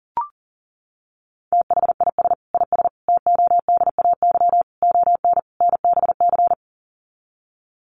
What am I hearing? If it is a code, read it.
Morse code, heard as a keyed tone, about 40 wpm, 700 Hz. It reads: THIS IS TODAY ON NBC
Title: Sets of 5 Words Encore 40wpm